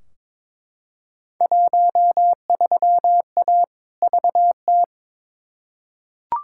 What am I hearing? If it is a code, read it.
Morse code, heard as a keyed tone, about 21 wpm, 700 Hz. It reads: 13A VT
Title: Field Day Exchanges Encore - 22wpm